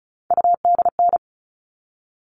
Morse code, heard as a keyed tone, about 34 wpm, 700 Hz. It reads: UDN
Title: All Cut Numbers - 3-Digit 35wpm